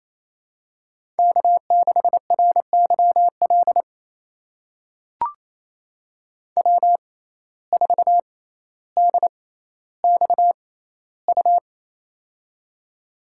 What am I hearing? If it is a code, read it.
Morse code, heard as a keyed tone, about 28 wpm, spaced as for 14 wpm, 700 Hz. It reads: K6RYL W4DXU
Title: USA Call Signs - 5 Characters - 6x Character Spacing - 28wpm